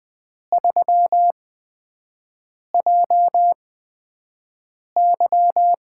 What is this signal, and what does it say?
Morse code, heard as a keyed tone, about 20 wpm, spaced as for 5 wpm, 700 Hz. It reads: 3JY
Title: USA Call Signs - 5 Characters - 8x Character Spacing - 20wpm